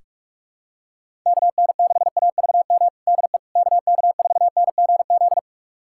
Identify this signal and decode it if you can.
Morse code, heard as a keyed tone, about 46 wpm, 700 Hz. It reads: KN6AVM DE KK4NGZ
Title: USA Callsign DE Callsign 45wpm